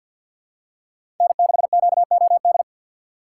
Morse code, heard as a keyed tone, about 48 wpm, 700 Hz. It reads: N6QOD
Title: USA Call Signs - 5 Characters - 6x Character Spacing - 50wpm